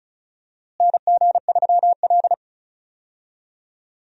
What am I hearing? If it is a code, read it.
Morse code, heard as a keyed tone, about 34 wpm, 700 Hz. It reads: NG3L
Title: USA Call Signs - 4 Characters - Repeated 2x - 35wpm